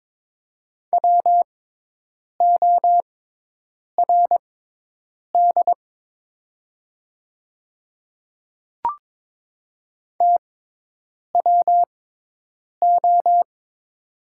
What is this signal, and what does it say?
Morse code, heard as a keyed tone, about 22 wpm, spaced as for 8 wpm, 700 Hz. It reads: WORD TWO
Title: 6x Character Spacing - Top 100 Words - 22wpm